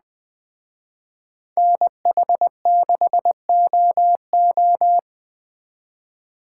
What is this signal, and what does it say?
Morse code, heard as a keyed tone, about 20 wpm, 700 Hz. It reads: NH6OO
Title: Speed Racing - USA 2x2 Call Signs 17wpm